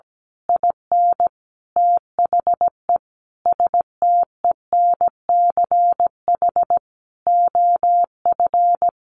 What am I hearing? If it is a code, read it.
Morse code, heard as a keyed tone, about 17 wpm, 700 Hz. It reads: IN THE STENCH OF
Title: A Sound Of Thunder 17wpm